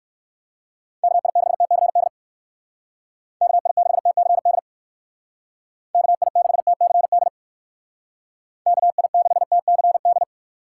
Morse code, heard as a keyed tone, about 59 wpm, 700 Hz. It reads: KI6TCD KI6TCD KI6TCD KI6TCD